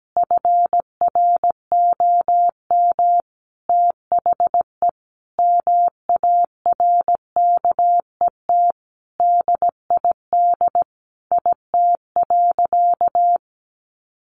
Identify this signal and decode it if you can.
Morse code, heard as a keyed tone, about 17 wpm, 700 Hz. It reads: FROM THE MARKET DID IT.
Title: Sentences from Top 600 Words 17wpm